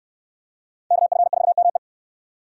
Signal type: Morse code

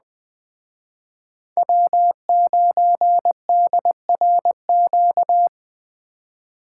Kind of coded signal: Morse code

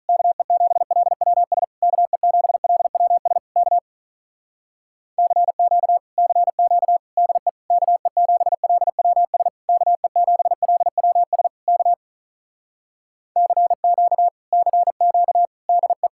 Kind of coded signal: Morse code

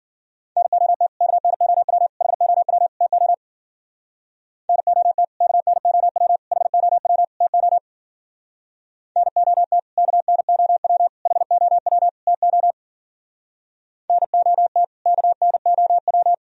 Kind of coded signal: Morse code

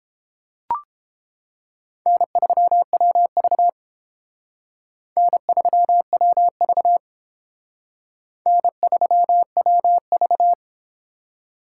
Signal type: Morse code

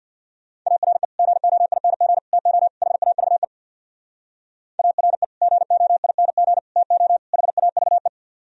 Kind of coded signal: Morse code

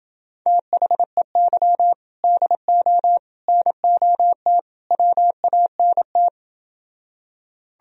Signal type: Morse code

